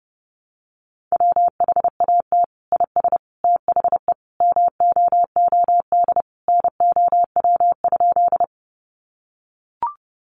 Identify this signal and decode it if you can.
Morse code, heard as a keyed tone, about 30 wpm, 700 Hz. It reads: WHAT IS THE MOOD NOW?
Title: Sentences from Top 1100 Words 30wpm